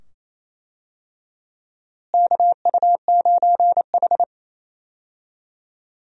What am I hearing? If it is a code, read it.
Morse code, heard as a keyed tone, about 28 wpm, 700 Hz. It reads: KU9H
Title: USA Call Signs - 4 Characters - Repeated 2x - 28wpm